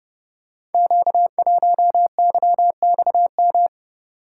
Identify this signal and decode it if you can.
Morse code, heard as a keyed tone, about 30 wpm, 700 Hz. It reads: Q1YXM
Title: Code Groups - 5 Letter:Numbers - 30wpm